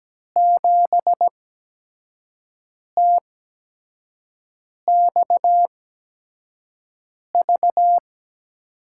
Morse code, heard as a keyed tone, about 17 wpm, spaced as for 5 wpm, 700 Hz. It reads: 7TXV
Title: USA Call Signs - 6 Characters - 8x Character Spacing - 17wpm